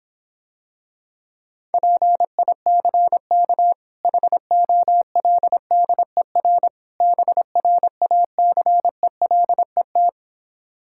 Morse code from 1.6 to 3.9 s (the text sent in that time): PICK